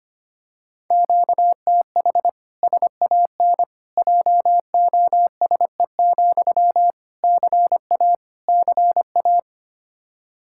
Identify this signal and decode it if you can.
Morse code, heard as a keyed tone, about 25 wpm, 700 Hz. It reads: QTH SAN JOSE, CA CA
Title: Practice QSO - Exchange 1 - Beginning - 25wpm